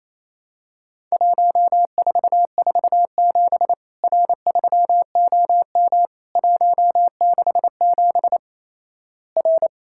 Morse code, heard as a keyed tone, about 27 wpm, 700 Hz. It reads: 1447 R3OM 167 R
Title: Worked All Europe DX Contest - QTC Exchange - 28wpm